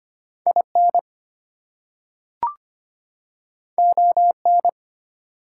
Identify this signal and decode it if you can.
Morse code, heard as a keyed tone, about 25 wpm, 700 Hz. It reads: IN ON
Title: Two Letter Words - 25wpm